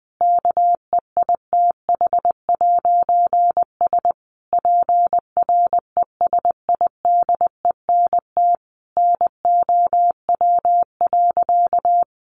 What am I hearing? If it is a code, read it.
Morse code, heard as a keyed tone, about 19 wpm, 700 Hz. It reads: KEITH'S PRESIDENT NOW.
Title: A Sound Of Thunder 20wpm